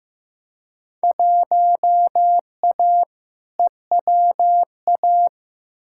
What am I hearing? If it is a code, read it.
Morse code, heard as a keyed tone, about 15 wpm, 700 Hz. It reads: 1A EWA
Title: Field Day Exchanges Encore - 15wpm